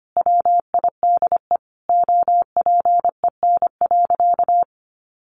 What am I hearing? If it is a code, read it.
Morse code, heard as a keyed tone, about 25 wpm, 700 Hz. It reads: WIDE OPEN.
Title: Sentences from Top 1500 Words 25wpm